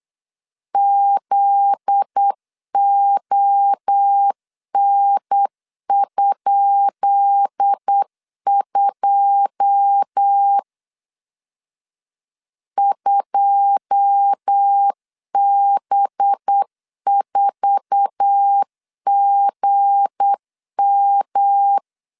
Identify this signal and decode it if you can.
Morse code, heard as a keyed tone, about 8 wpm, 800 Hz. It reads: ZON?2 2B4GM
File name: LEZIONI / 10 SIGNS / 10B INT 8wpm 3534